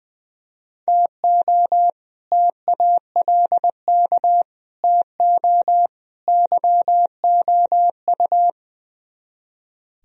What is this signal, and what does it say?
Morse code, heard as a keyed tone, about 21 wpm, 700 Hz. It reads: TO TALK TO YOU
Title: Speed-Racing - Sets of 4 Words 20wpm